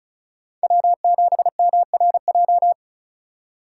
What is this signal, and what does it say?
Morse code, heard as a keyed tone, about 47 wpm, spaced as for 41 wpm, 700 Hz. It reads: W7MRJ
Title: Speed Racing - USA Call Signs 35wpm